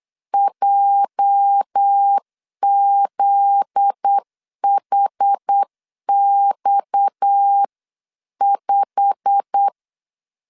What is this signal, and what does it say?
Morse code, heard as a keyed tone, about 8 wpm, 800 Hz. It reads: JZHX5
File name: CW / DET / 1 9 1500C S A 8wpm 5637